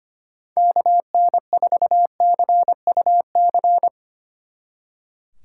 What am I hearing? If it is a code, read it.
Morse code, heard as a keyed tone, about 31 wpm, 700 Hz. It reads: KN4CUC
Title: Speed Racing - USA 2x3 Call Signs 25wpm